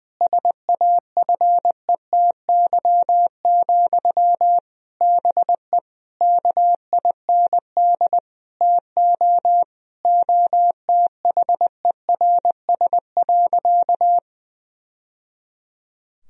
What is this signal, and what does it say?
Morse code, heard as a keyed tone, about 20 wpm, 700 Hz. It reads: SAFETY, BE KIND TO OTHERS.
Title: Sentences from Top 1000 Words - Review Part 4 - 20wpm